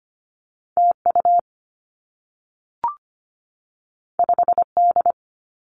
Morse code, heard as a keyed tone, about 25 wpm, 700 Hz. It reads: TU 5D
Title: Two Challenging Letter-Number Combinations 25wpm